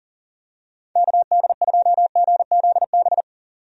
Morse code, heard as a keyed tone, about 40 wpm, 700 Hz. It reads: KD2GZB
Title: USA Call Signs - 6 Characters - 40wpm